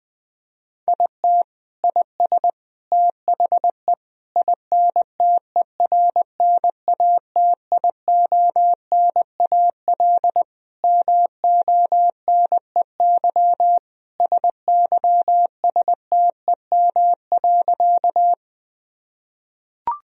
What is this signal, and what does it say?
Morse code, heard as a keyed tone, about 20 wpm, 700 Hz. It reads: IT IS THE INTERNATIONAL MONEY SYSTEM.
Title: Sentences from Top 1000 Words - Review Part 4 - 20wpm